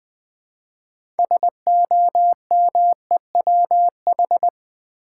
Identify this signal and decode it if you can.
Morse code, heard as a keyed tone, about 20 wpm, 700 Hz. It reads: SOMEWH
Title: Build A Word - 2 Letters At A Time - 8-Letter Words - 20wpm